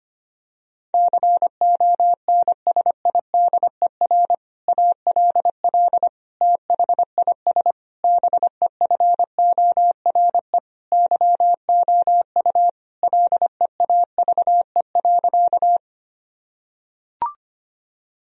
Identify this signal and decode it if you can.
Morse code, heard as a keyed tone, about 25 wpm, 700 Hz. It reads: CONSIDER ALL THIS BEFORE YOU LEAVE.
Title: Sentences from Top 1000 Words - Review Part 4 - 25wpm